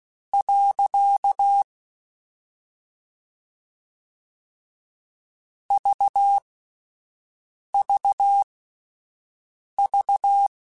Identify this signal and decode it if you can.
Morse code, heard as a keyed tone, about 16 wpm, spaced as for 5 wpm, 800 Hz. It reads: . VVV
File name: Misc / Misc-mp3s / Punct @5